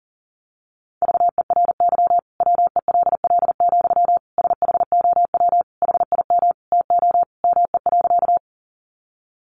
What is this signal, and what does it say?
Morse code, heard as a keyed tone, about 40 wpm, 700 Hz. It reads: VERY WELL, SHOW HIM TO ME.
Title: Sentences from Top 200 Words 40wpm